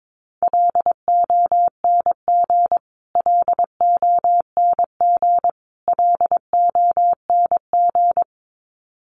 Morse code, heard as a keyed tone, about 22 wpm, 700 Hz. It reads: LONG LONG LONG
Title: Top 300 Words Repeated 22wpm